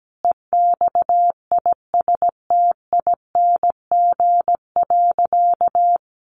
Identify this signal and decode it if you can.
Morse code, heard as a keyed tone, about 17 wpm, 700 Hz. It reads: EXISTING.
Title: Sentences from Top 1500 Words 17wpm